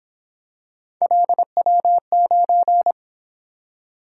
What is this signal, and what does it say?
Morse code, heard as a keyed tone, about 27 wpm, 700 Hz. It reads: LW9
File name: Speed-Racing - Three Letter-Number Combinations - 22wpm